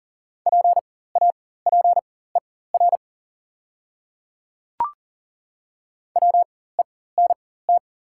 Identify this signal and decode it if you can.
Morse code, heard as a keyed tone, about 38 wpm, spaced as for 18 wpm, 700 Hz. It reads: PAPER WENT
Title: 4x Character Spacing - Top 300 Words - 40wpm